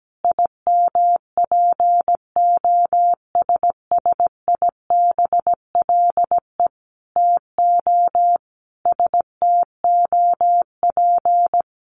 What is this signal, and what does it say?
Morse code, heard as a keyed tone, about 17 wpm, 700 Hz. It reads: IMPOSSIBLE TO STOP
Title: Sentences from Top 1100 Words 17wpm